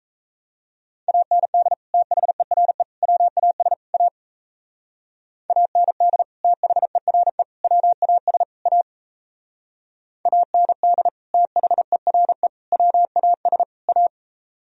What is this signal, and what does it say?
Morse code, heard as a keyed tone, about 36 wpm, 700 Hz. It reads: AND THERE WAS A AND THERE WAS A AND THERE WAS A